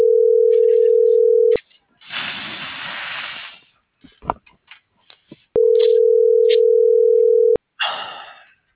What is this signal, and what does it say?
Ringback tone (North America), 440 + 480 Hz.